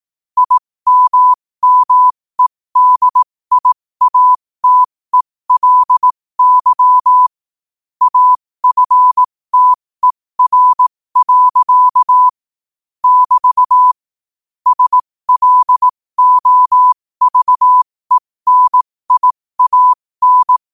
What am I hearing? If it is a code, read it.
Morse code, heard as a keyed tone, about 18 wpm, spaced as for 15 wpm, 1000 Hz. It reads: IMMEDIATELY AFTER. <BT> SLOVENIAN
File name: qotdfiles / 15wpm